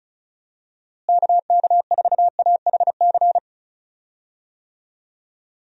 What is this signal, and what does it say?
Morse code, heard as a keyed tone, about 34 wpm, 700 Hz. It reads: KK4AHC
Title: USA Call Signs - 6 Characters - Repeated 3x - 35wpm